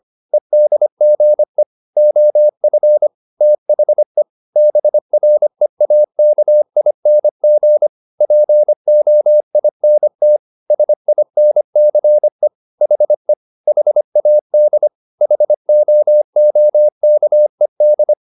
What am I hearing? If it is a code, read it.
Morse code, heard as a keyed tone, about 25 wpm, 600 Hz. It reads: EDGE OF THE BREAKING POINT SINCE HE HAD HOOKED
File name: the old man and the sea part 1 25wpm